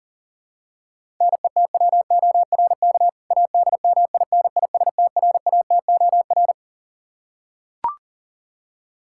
Morse code, heard as a keyed tone, about 37 wpm, 700 Hz. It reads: NETWORK ADMINISTRATOR
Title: K1USN SST QSO Practice 40wpm